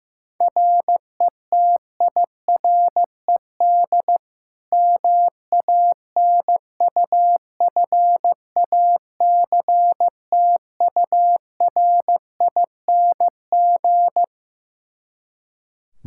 Morse code, heard as a keyed tone, about 15 wpm, 700 Hz. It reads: RETIRED MANUFACTURING
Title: K1USN SST QSO Practice 15wpm